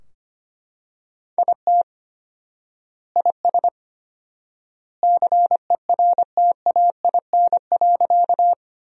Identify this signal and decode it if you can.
Morse code, heard as a keyed tone, about 25 wpm, 700 Hz. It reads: IT IS CERTAIN.
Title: 4x Word-Spacing - Sentences from Top 500 Words - 25wpm